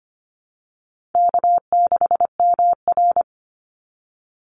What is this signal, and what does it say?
Morse code, heard as a keyed tone, about 25 wpm, 700 Hz. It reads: K6MR
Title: Easy Call Signs Repeated 2x - 25wpm